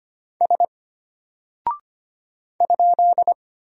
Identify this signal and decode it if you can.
Morse code, heard as a keyed tone, about 25 wpm, 700 Hz. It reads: S ?
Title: Single Character (Letter:Number:Punctuation:Prosign) Alt 25wpm